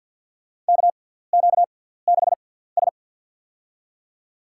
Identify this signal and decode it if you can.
Morse code, heard as a keyed tone, about 49 wpm, spaced as for 17 wpm, 700 Hz. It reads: KQ6S
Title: USA Call Signs - 4 Characters - 6x Character Spacing - 50wpm